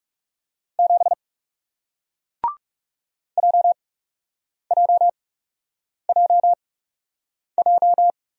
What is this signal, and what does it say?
Morse code, heard as a keyed tone, about 40 wpm, spaced as for 8 wpm, 700 Hz. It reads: 7 JJJJ